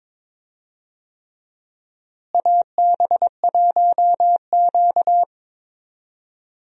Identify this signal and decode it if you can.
Morse code, heard as a keyed tone, about 22 wpm, 700 Hz. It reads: AB1Q
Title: USA Call Signs - 4 Characters - Repeated 3x - 22wpm